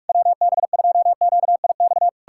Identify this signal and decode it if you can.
Morse code, heard as a keyed tone, about 59 wpm, 700 Hz. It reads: WB2QIX